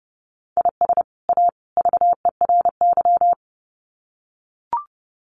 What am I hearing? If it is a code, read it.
Morse code, heard as a keyed tone, about 30 wpm, 700 Hz. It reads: IS A VERY
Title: Sets of 3 Words 30wpm